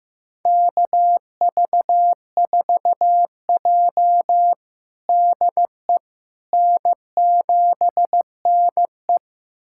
Morse code, heard as a keyed tone, about 15 wpm, 700 Hz. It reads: KV4J DE N7NE
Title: USA Callsign DE Callsign - 4 Characters - 15wpm